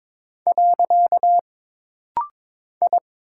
Morse code, heard as a keyed tone, about 22 wpm, 700 Hz. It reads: . I
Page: This rapid-fire timing is geared to push your ICR (Instant Character Recognition) skills.